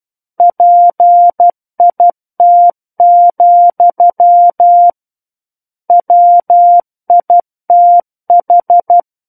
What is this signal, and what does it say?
Morse code, heard as a keyed tone, about 12 wpm, 700 Hz. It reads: PIT, WITH
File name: Book One / 03-Chapter 3 - 12 WPM